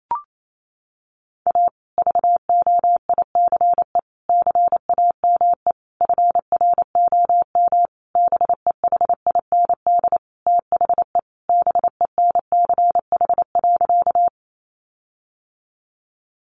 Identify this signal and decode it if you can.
Morse code, heard as a keyed tone, about 28 wpm, 700 Hz. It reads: A VOICE CAME FROM BEHIND THE BENCH.
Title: Sentences from Top 1400 Words 28wpm